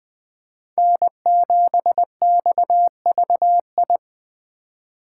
Morse code, heard as a keyed tone, about 23 wpm, 700 Hz. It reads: N7XVI
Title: Speed Racing - USA 1x3 Call Signs 20wpm